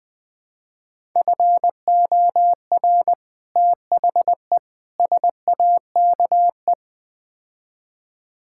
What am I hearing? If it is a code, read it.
Morse code, heard as a keyed tone, about 20 wpm, 700 Hz. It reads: FOR THE SAKE
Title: Build A Phrase - 4-Word Phrases - 20wpm